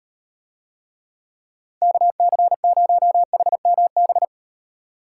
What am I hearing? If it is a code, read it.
Morse code, heard as a keyed tone, about 38 wpm, 700 Hz. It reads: KC0HMB